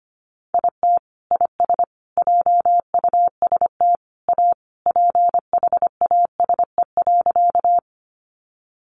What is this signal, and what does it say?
Morse code, heard as a keyed tone, about 25 wpm, 700 Hz. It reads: IT IS JUST A PHASE.
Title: Sentences from Top 1200 Words 25wpm